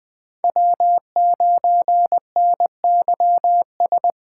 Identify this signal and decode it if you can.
Morse code, heard as a keyed tone, about 20 wpm, 700 Hz. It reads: W9NYS
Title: Speed Racing - USA 1x3 Call Signs 17wpm